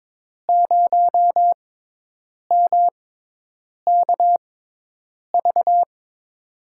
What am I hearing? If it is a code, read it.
Morse code, heard as a keyed tone, about 22 wpm, spaced as for 8 wpm, 700 Hz. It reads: 0MKV
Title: USA Call Signs - 6 Characters - 6x Character Spacing - 22wpm